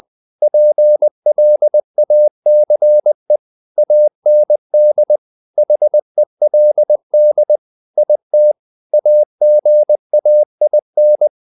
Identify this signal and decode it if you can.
Morse code, heard as a keyed tone, about 20 wpm, 600 Hz. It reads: PLACE AND HELD IT AGAIN
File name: the old man and the sea part 2 20wpm